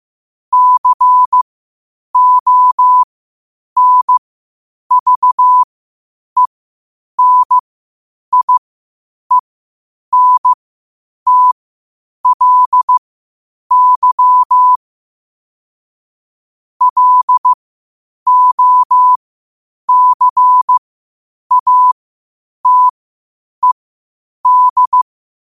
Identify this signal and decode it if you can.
Morse code, heard as a keyed tone, about 15 wpm, spaced as for 8 wpm, 1000 Hz. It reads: CONVENIENTLY LOCATED
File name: qotdfiles / 8wpm